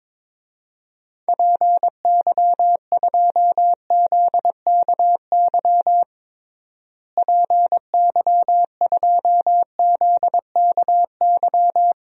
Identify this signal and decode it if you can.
Morse code, heard as a keyed tone, about 22 wpm, 700 Hz. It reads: PY2ZKY PY2ZKY
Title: DXCC-100 Call Signs - 6 Characters - Repeated 3x - 22wpm